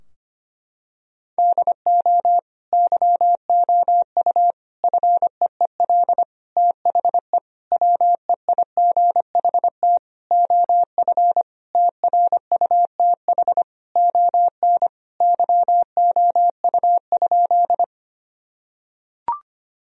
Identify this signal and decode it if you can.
Morse code, heard as a keyed tone, about 25 wpm, 700 Hz. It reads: DO YOU FEEL THE WEIGHT OF TRUTH ON YOU?
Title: Sentences from Top 1000 Words - Review Part 3 - 25wpm